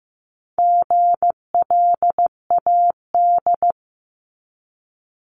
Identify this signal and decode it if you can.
Morse code, heard as a keyed tone, about 15 wpm, 700 Hz. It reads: GLAD
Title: Multi-letter Phonemes 15wpm